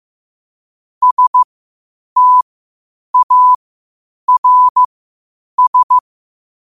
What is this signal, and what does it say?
Morse code, heard as a keyed tone, about 15 wpm, spaced as for 8 wpm, 1000 Hz. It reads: STARS
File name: qotdfiles / 8wpm